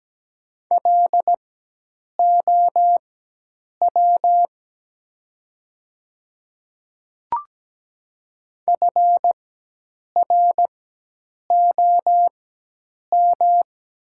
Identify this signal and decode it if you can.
Morse code, heard as a keyed tone, about 17 wpm, spaced as for 8 wpm, 700 Hz. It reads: LOW FROM
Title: 4x Character Spacing - Top 300 Words - 17wpm